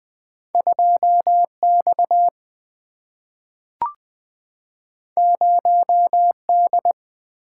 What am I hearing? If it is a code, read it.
Morse code, heard as a keyed tone, about 20 wpm, 700 Hz. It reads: 2X 0D
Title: Two Letter-Number Combinations - 20wpm